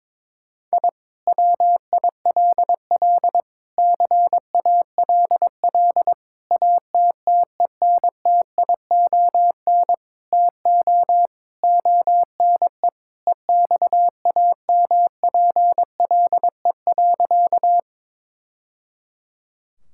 Morse code, heard as a keyed tone, about 22 wpm, 700 Hz. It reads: I WILL CALL ATTENTION TO ONE EXAMPLE.
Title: Sentences from Top 1000 Words - Review Part 6 - 22wpm